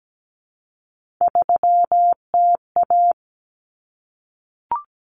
Morse code, heard as a keyed tone, about 17 wpm, 700 Hz. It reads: 3TA
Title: Common Cut Numbers - 3-Digit 17wpm